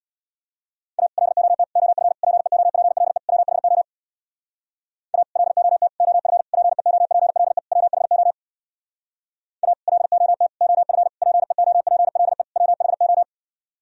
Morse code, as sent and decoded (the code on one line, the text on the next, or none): .- / .-.. --- - / --- ..-. / .--. --- .--. .-.. / .-- .... --- / .- / .-.. --- - / --- ..-. / .--. --- .--. .-.. / .-- .... --- / .- / .-.. --- - / --- ..-. / .--. --- .--. .-.. / .-- .... ---
A LOT OF PEOPLE WHO A LOT OF PEOPLE WHO A LOT OF PEOPLE WHO